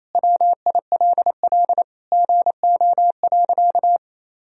Morse code, heard as a keyed tone, about 28 wpm, 700 Hz. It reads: WILL GO.
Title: Sentences from Top 1000 Words - Review Part 1 - 28wpm